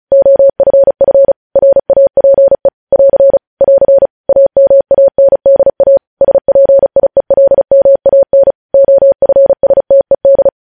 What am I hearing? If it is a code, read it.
Morse code, heard as a keyed tone, about 36 wpm, 550 Hz. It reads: OFF RAPE <AR> <AR> AMANDA SPIELMAN OFSTED
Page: News headlines in Morse code at 35 wpm